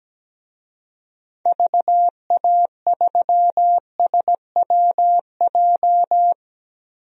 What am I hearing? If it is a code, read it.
Morse code, heard as a keyed tone, about 17 wpm, 700 Hz. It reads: VA3SWJ
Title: DXCC-100 Call Signs - 6 Characters - 17wpm